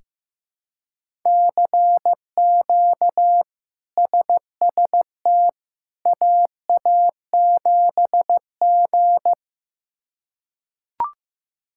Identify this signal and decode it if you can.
Morse code, heard as a keyed tone, about 16 wpm, 700 Hz. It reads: CQ SST AA7G
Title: K1USN SST QSO Practice - 15wpm